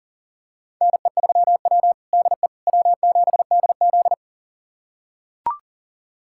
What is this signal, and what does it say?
Morse code, heard as a keyed tone, about 40 wpm, 700 Hz. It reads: NE3W DE W7DZ
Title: USA Callsign DE Callsign - 4 Characters - 40wpm